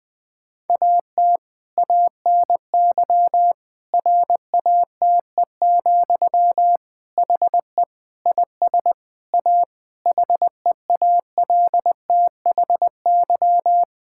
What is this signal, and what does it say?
Morse code, heard as a keyed tone, about 20 wpm, 700 Hz. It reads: AT ANY RATE, HE IS A HEALTHY
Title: Sentences from Top 1000 Words - Review Part 5 - 20wpm